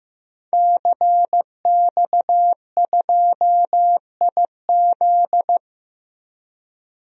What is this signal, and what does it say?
Morse code, heard as a keyed tone, about 15 wpm, 700 Hz. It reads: CX2IZ
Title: DXCC-100 Call Signs - 5 Characters - Repeated 2x - 15wpm